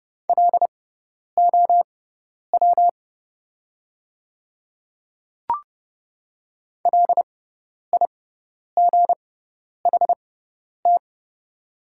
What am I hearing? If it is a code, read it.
Morse code, heard as a keyed tone, about 30 wpm, spaced as for 10 wpm, 690 Hz. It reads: LOW LIGHT
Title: 6x Character Spacing - Top 200 Words - 30wpm